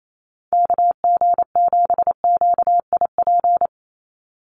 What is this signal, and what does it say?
Morse code, heard as a keyed tone, about 28 wpm, 700 Hz. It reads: KG7QIP
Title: USA Call Signs 28wpm